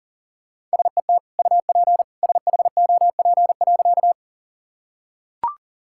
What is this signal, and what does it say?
Morse code, heard as a keyed tone, about 40 wpm, 700 Hz. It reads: SET UP SHOP.
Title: Sentences from Top 1600 Words 40wpm